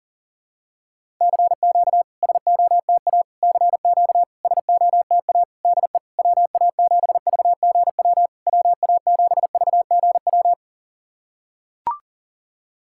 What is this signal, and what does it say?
Morse code, heard as a keyed tone, about 40 wpm, 690 Hz. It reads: CQ SOTA CQ SOTA DE WA7VGW WA7VGW
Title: SOTA QSO Practice 40wpm